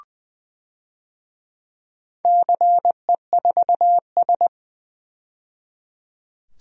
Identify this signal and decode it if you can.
Morse code, heard as a keyed tone, about 20 wpm, 700 Hz. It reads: CE4S
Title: DXCC-100 Call Signs - 4 Characters - 20wpm